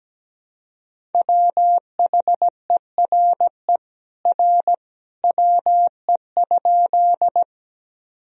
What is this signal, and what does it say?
Morse code, heard as a keyed tone, about 17 wpm, 700 Hz. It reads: WHERE R WE?
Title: Sentences from Top 200 Words and Abbreviations 17wpm